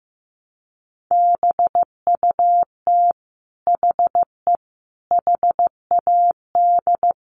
Morse code, heard as a keyed tone, about 15 wpm, 700 Hz. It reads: BUT HE HAD
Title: Sentences from Top 100 Words 15wpm